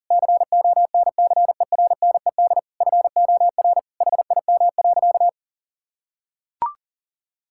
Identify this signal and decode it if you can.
Morse code, heard as a keyed tone, about 37 wpm, 700 Hz. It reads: CONCERNED FOR HIM.
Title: Sentences from Top 1000 Words - Review Part 6 - 40wpm